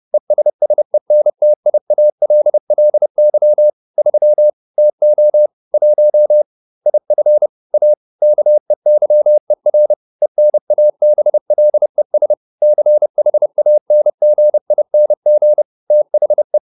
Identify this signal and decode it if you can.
Morse code, heard as a keyed tone, about 30 wpm, 600 Hz. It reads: ESSENTIALLY 3 TO 1 IF A KEYER ENABLES CHANGING THE